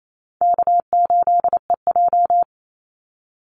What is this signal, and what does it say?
Morse code, heard as a keyed tone, about 28 wpm, 700 Hz. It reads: K8EJ
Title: Easy Call Signs - 28wpm